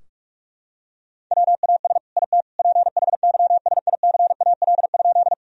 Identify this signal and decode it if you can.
Morse code, heard as a keyed tone, about 46 wpm, 700 Hz. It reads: WAS IT PHYSICAL?
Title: Sentences from Top 1000 Words - Review Part 4 - 45wpm